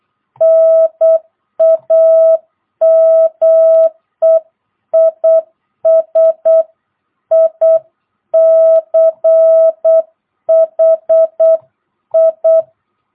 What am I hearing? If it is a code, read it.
Morse code, heard as a keyed tone, about 8 wpm, 660 Hz. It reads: NAME IS ICHI
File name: jh8cbh / 2022cw